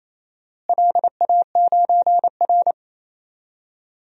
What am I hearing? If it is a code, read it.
Morse code, heard as a keyed tone, about 28 wpm, 700 Hz. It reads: LA9R
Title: DXCC-100 Call Signs - 4 Characters - Repeated 2x - 28wpm